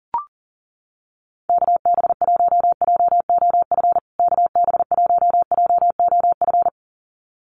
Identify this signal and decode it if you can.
Morse code, heard as a keyed tone, about 40 wpm, 700 Hz. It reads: KB1JOF KB1JOF
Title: USA Call Signs Repeated 2x 40wpm